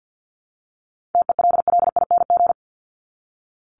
Morse code, heard as a keyed tone, about 48 wpm, 700 Hz. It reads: TELLING
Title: ING Suffix - Top 100 Words with Base Words 50wpm